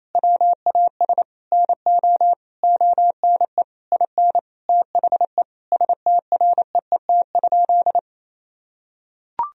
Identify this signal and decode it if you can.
Morse code, heard as a keyed tone, about 27 wpm, 700 Hz. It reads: WAS NO ONE IN THE STREET?
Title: Sentences from Top 1000 Words - Review Part 3 - 28wpm